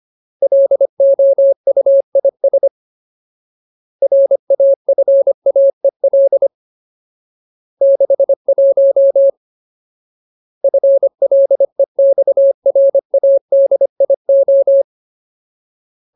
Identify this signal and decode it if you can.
Morse code, heard as a keyed tone, about 25 wpm, 550 Hz. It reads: LOUIS RAFAEL 61 FLEXRADIO
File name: Ultimate Word List / 25WPM Ultimate